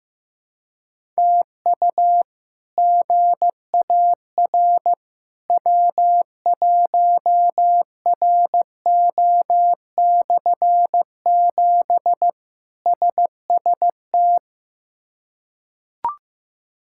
Morse code, heard as a keyed tone, about 15 wpm, 700 Hz. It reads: TU GAR W1RO/7 SST
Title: K1USN SST QSO Practice - 15wpm